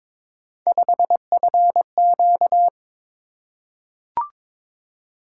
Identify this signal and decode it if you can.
Morse code, heard as a keyed tone, about 22 wpm, 700 Hz. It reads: 5FQ
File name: practice / 22wpm / Three Letter-Number Combinations - 22wpm